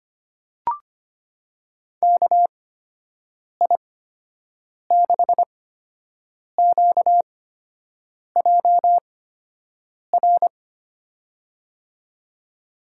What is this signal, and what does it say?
Morse code, heard as a keyed tone, about 25 wpm, spaced as for 7 wpm, 700 Hz. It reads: KI6QJR
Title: USA Call Signs - 8x Character Spacing - 25wpm